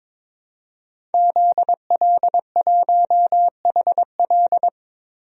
Morse code, heard as a keyed tone, about 22 wpm, 700 Hz. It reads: ZL1HL
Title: DXCC-100 Call Signs - 5 Characters - 22wpm